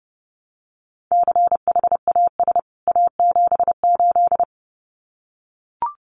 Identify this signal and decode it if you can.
Morse code, heard as a keyed tone, about 30 wpm, 700 Hz. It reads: CHAS A78
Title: CWT Exchange 30wpm